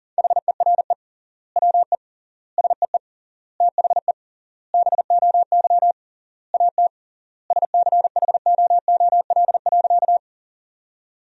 Morse code, heard as a keyed tone, about 40 wpm, 700 Hz. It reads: HERE WE SEE THE BOY AT SCHOOL.
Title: 3x Word-Spacing - Sentences from Top 300 Words - 40wpm